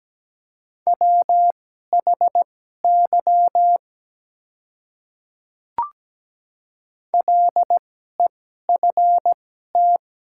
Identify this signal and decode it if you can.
Morse code, heard as a keyed tone, about 17 wpm, spaced as for 12 wpm, 700 Hz. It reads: WHY LEFT
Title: 2x Character Spacing - Top 300 Words - 17wpm